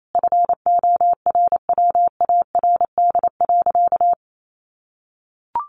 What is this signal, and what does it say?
Morse code, heard as a keyed tone, about 28 wpm, 700 Hz. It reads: FORWARD.
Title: Sentences from Top 1300 Words 28wpm